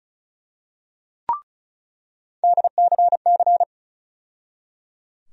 Morse code, heard as a keyed tone, about 36 wpm, 700 Hz. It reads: DCC
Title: USA Call Sign Postfixes - 3 Letters 35wpm